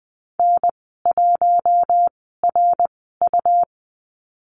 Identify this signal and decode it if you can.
Morse code, heard as a keyed tone, about 20 wpm, spaced as for 15 wpm, 700 Hz. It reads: N1RU
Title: 167 Lesson 27g - Number 7 - Call Signs 20wpm